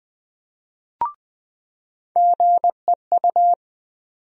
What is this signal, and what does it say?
Morse code, heard as a keyed tone, about 20 wpm, 700 Hz. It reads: GEU
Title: USA Call Sign Postfixes - 3 Letters 20wpm